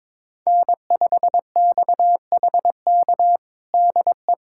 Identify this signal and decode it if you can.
Morse code, heard as a keyed tone, about 22 wpm, 700 Hz. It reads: N5XHK DE
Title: USA Callsign DE Callsign - 5 Characters - 22wpm